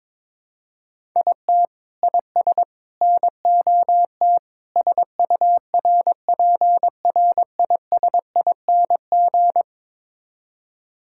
Morse code, heard as a keyed tone, about 22 wpm, 700 Hz. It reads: IT IS NOT SURPRISING
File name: practice / 22wpm / Build A Phrase - 5-Word Phrases - 22wpm